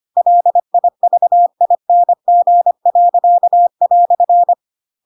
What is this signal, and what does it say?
Morse code, heard as a keyed tone, about 25 wpm, 700 Hz. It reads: LIVING."
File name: Pinocchio / Pinocchio ch3 25 25 ik1hge